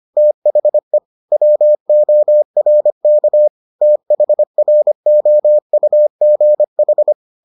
Morse code, heard as a keyed tone, about 25 wpm, 600 Hz. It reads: THE WORK THROUGH